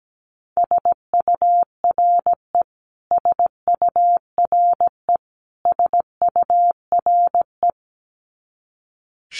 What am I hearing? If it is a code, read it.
Morse code, heard as a keyed tone, about 17 wpm, 700 Hz. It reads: SURE SURE SURE
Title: Top 300 Words Repeated 17wpm